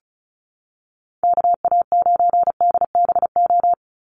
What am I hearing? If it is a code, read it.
Morse code, heard as a keyed tone, about 34 wpm, 700 Hz. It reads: KA9DBO
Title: USA Call Signs 35wpm